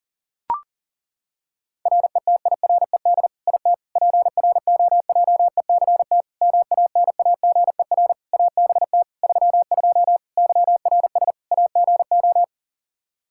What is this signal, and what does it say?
Morse code, heard as a keyed tone, about 40 wpm, 700 Hz. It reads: RETIRED IT PROJECT MANAGER ABT 32 YRS AGO
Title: Practice QSO - Exchange 3 - More Info - 40wpm